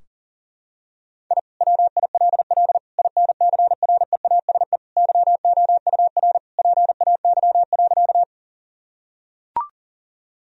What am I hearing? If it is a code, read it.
Morse code, heard as a keyed tone, about 40 wpm, 700 Hz. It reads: I WILL INCREASE YOUR PAY.
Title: Sentences from Top 1000 Words - Review Part 4 - 40wpm